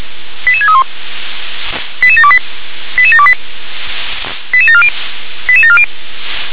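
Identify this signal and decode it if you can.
Five-tone selective call (ZVEI), heard as quick runs of five tones, tones of about 70 ms. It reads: 88631 88728 88728 88740 88740
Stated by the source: Received on SDR, ZVEI five-tone selective call.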